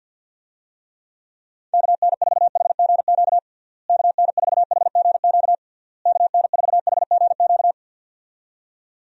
Morse code, heard as a keyed tone, about 50 wpm, 700 Hz. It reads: KN4HGQ KN4HGQ KN4HGQ
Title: USA Call Signs - 6 Characters - Repeated 3x - 50wpm